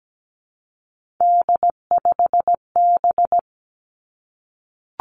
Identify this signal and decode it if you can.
Morse code, heard as a keyed tone, about 17 wpm, 700 Hz. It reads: D5B
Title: Three Challenging Letter-Number Combinations 17wpm